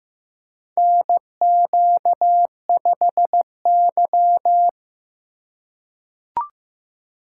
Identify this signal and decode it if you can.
Morse code, heard as a keyed tone, about 15 wpm, 700 Hz. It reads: NQ5Y
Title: USA Call Signs - 4 Characters - 4x Character Spacing - 15wpm